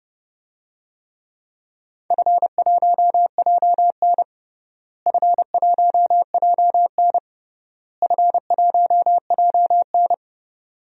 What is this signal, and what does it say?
Morse code, heard as a keyed tone, about 29 wpm, 700 Hz. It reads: F1JN F1JN F1JN
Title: DXCC-100 Call Signs - 4 Characters - Repeated 3x - 30wpm